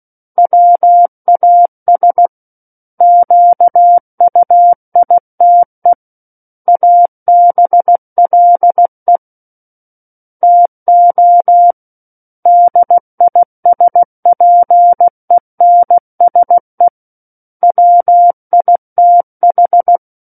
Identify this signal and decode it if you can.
Morse code, heard as a keyed tone, about 16 wpm, 700 Hz. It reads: WAS QUITE ABLE TO DISPENSE WITH
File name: Book One / 07-Chapter 7 - 16 WPM